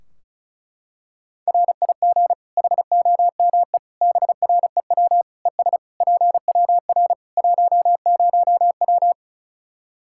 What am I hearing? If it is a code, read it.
Morse code, heard as a keyed tone, about 36 wpm, 700 Hz. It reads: RIG HOME BREW ES PWR 10W
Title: Practice QSO - Exchange 2 - Info - 35wpm